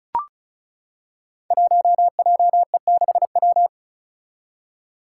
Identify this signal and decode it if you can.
Morse code, heard as a keyed tone, about 36 wpm, 700 Hz. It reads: 1JE6W
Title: Five Challenging Letter-Number Combinations 35wpm